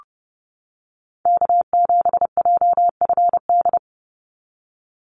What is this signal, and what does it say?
Morse code, heard as a keyed tone, about 30 wpm, 700 Hz. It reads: K7JFD
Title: NA SOTA Activator Callsigns 30wpm